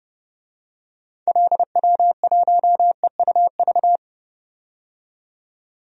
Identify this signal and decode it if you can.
Morse code, heard as a keyed tone, about 30 wpm, 700 Hz. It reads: LW1EUV
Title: DXCC-100 Call Signs - 6 Characters - 30wpm